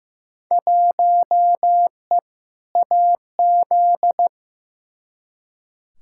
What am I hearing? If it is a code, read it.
Morse code, heard as a keyed tone, about 15 wpm, 700 Hz. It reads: 1E AZ
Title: Field Day Exchanges - 15wpm